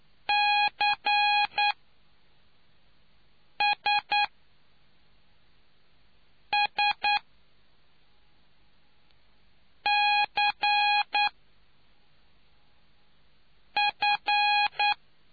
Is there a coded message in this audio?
Yes: Morse code from 0.3 to 14.9 s, heard as a keyed tone, about 9 wpm, spaced as for 3 wpm, 830 Hz. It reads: CSSCF